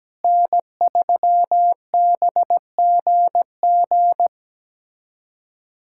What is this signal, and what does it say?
Morse code, heard as a keyed tone, about 17 wpm, 700 Hz. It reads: N3BGG
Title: USA Callsign DE Callsign 17wpm